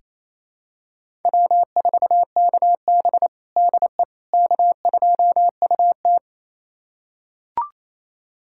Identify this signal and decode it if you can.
Morse code, heard as a keyed tone, about 28 wpm, 700 Hz. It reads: W4KB DE K2UT
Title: USA Callsign DE Callsign - 4 Characters - 28wpm